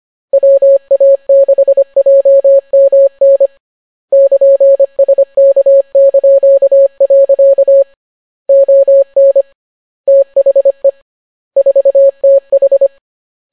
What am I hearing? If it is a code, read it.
Morse code, heard as a keyed tone, about 25 wpm, 550 Hz. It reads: WA6JMN <KN>SK). ON THE 4TH